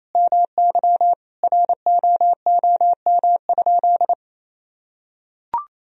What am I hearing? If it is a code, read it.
Morse code, heard as a keyed tone, about 28 wpm, 700 Hz. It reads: MY ROOM?
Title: Sentences from Top 1000 Words - Review Part 1 - 28wpm